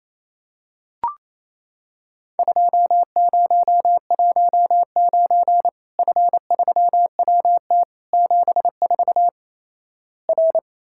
Morse code, heard as a keyed tone, about 28 wpm, 700 Hz. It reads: 2019 F3WT 74 R
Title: Worked All Europe DX Contest - QTC Exchange - 28wpm